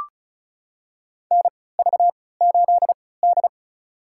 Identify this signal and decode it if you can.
Morse code, heard as a keyed tone, about 36 wpm, spaced as for 20 wpm, 700 Hz. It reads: NV8D
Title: USA Call Signs - 4 Characters - 3x Character Spacing - 35wpm